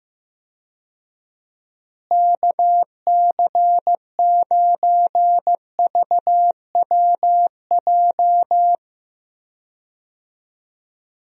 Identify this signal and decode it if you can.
Morse code, heard as a keyed tone, about 15 wpm, 700 Hz. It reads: KC9VWJ
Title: USA Call Signs - 6 Characters - Repeated 3x - 15wpm